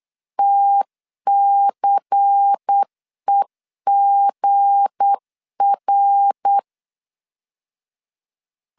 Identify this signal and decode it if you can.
Morse code, heard as a keyed tone, about 8 wpm, 800 Hz. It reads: TCEGR
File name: LEZIONI / 7 GF9Z / 7 G 8wpm 3118